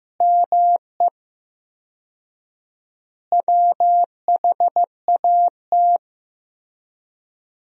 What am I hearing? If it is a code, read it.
Morse code, heard as a keyed tone, about 15 wpm, 700 Hz. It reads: ME WHAT
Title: 4x Word-Spacing - Sentences from Top 200 Words - 15wpm